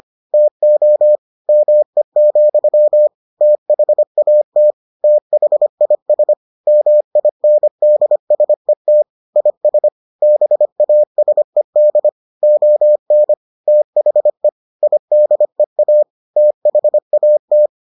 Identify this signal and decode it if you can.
Morse code, heard as a keyed tone, about 25 wpm, 600 Hz. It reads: TO ME, THAT THIS MINDSET IS BASED ON THE IDEA THAT